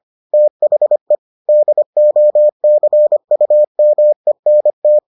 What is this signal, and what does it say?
Morse code, heard as a keyed tone, about 25 wpm, 600 Hz. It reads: THE DOCUMENT